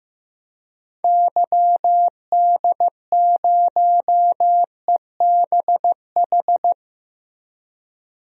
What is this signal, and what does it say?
Morse code, heard as a keyed tone, about 15 wpm, 700 Hz. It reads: YD0EBH
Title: DXCC-100 Call Signs - 6 Characters - 15wpm